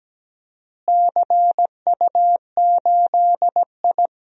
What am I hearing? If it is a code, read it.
Morse code, heard as a keyed tone, about 17 wpm, 700 Hz. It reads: CU8I
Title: DXCC-100 Call Signs - 4 Characters - Repeated 3x - 17wpm